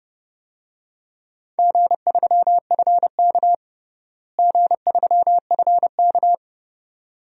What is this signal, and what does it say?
Morse code, heard as a keyed tone, about 30 wpm, 700 Hz. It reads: G3FK G3FK
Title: DXCC-100 Call Signs - 4 Characters - Repeated 2x - 30wpm